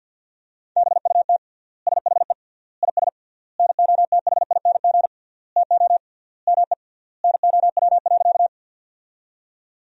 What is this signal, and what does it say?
Morse code, heard as a keyed tone, about 49 wpm, 700 Hz. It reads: BUT SHE IS NOTHING TO ME NOW.
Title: 3x Word-Spacing - Sentences from Top 500 Words - 50wpm